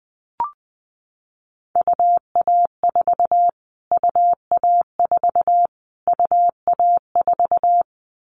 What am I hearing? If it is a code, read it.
Morse code, heard as a keyed tone, about 20 wpm, 700 Hz. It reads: UA4 UA4 UA4
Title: Callsign Prefixes for Top 25 DXCC - Repeated 3x - 20wpm